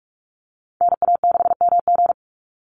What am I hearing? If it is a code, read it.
Morse code, heard as a keyed tone, about 44 wpm, 700 Hz. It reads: NA6MG
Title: NA SOTA Activator Callsigns 45wpm